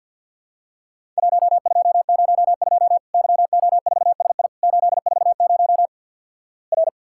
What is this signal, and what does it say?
Morse code, heard as a keyed tone, about 50 wpm, 700 Hz. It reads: 1202 YO4SI 840 R
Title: Worked All Europe DX Contest - QTC Exchange with Repeats - 50wpm